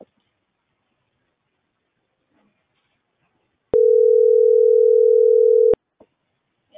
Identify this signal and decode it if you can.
Ringback tone (North America), 440 + 480 Hz.